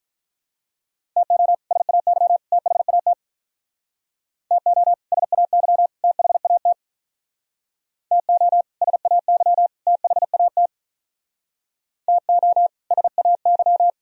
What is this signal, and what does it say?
Morse code, heard as a keyed tone, about 45 wpm, 700 Hz. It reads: TO SAY THAT TO SAY THAT TO SAY THAT TO SAY